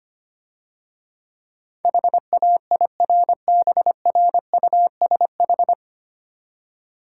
Morse code, heard as a keyed tone, about 26 wpm, 700 Hz. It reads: HAIRBRUSH
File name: Easy Words - Speed-Racing with Hints - 25wpm